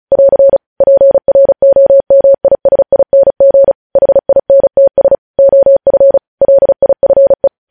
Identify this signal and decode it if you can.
Morse code, heard as a keyed tone, about 36 wpm, 550 Hz. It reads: <AR> PROMISING HINTS OF LIFE
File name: podcast / News headlines-1x-35x35